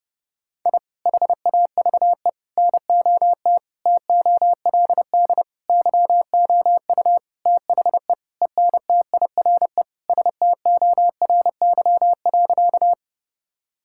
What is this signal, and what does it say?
Morse code, heard as a keyed tone, about 30 wpm, 700 Hz. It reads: I HAVE NOT TOLD YOU THE ENTIRE STORY.
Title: Sentences from Top 1000 Words - Review Part 5 - 30wpm